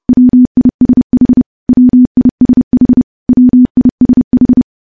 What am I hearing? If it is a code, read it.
Morse code, heard as a keyed tone, about 30 wpm, 250 Hz. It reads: WISH WISH WISH
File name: morse / morse-training-30WPM